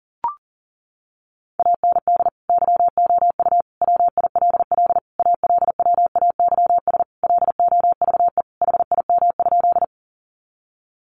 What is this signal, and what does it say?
Morse code, heard as a keyed tone, about 40 wpm, 700 Hz. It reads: AND YOU WILL ALWAYS LOVE HIM?
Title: Sentences from Top 400 Words 40wpm